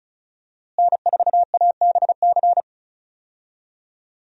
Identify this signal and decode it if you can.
Morse code, heard as a keyed tone, about 36 wpm, 700 Hz. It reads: N4ABC
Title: USA Call Signs - 5 Characters - Repeated 3x - 35wpm